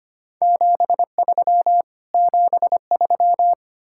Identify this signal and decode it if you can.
Morse code, heard as a keyed tone, about 33 wpm, 700 Hz. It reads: 73 73